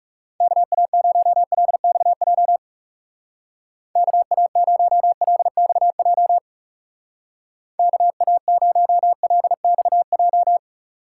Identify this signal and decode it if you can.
Morse code, heard as a keyed tone, about 35 wpm, 700 Hz. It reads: KA0LXJ KA0LXJ KA0LXJ
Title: Speed Racing - USA 2x3 Call Signs 30wpm